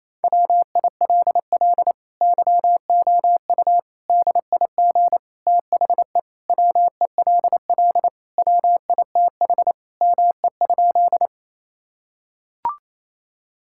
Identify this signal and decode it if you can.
Morse code, heard as a keyed tone, about 28 wpm, 700 Hz. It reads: WILL YOU DIG THE WELL WITH ME?
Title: Sentences from Top 1600 Words 28wpm